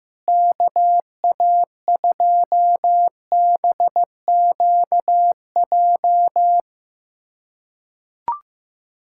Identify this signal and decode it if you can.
Morse code, heard as a keyed tone, about 15 wpm, 700 Hz. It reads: KA2BQJ
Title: USA Callsign DE Callsign - 6 Characters - 15wpm